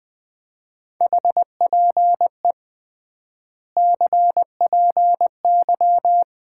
Morse code, heard as a keyed tone, about 20 wpm, 700 Hz. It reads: HPE CPY
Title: Common QSO Phrases - 3x Word-Spacing - 20wpm